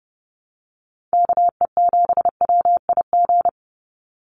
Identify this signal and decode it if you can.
Morse code, heard as a keyed tone, about 30 wpm, 700 Hz. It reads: KE7WIG
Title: USA Call Signs 30wpm